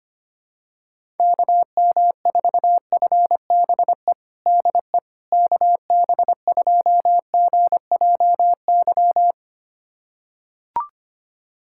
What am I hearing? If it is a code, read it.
Morse code, heard as a keyed tone, about 25 wpm, 700 Hz. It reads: KM4FBE DE KB2GJY
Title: USA Callsign DE Callsign - 6 Characters - 25wpm